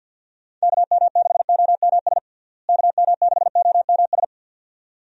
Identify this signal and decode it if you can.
Morse code, heard as a keyed tone, about 48 wpm, 700 Hz. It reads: KM6OMS KM6OMS
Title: USA Call Signs - 6 Characters - Repeated 2x - 50wpm